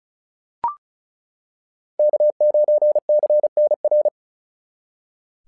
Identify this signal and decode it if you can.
Morse code, heard as a keyed tone, about 33 wpm, 690 Hz. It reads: K9CNR
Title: SOTA QSO Practice 35wpm